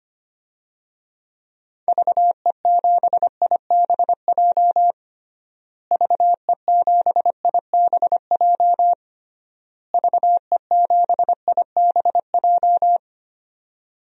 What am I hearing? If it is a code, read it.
Morse code, heard as a keyed tone, about 25 wpm, 700 Hz. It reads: VE7IBJ VE7IBJ VE7IBJ
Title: DXCC-100 Call Signs - 6 Characters - Repeated 3x - 25wpm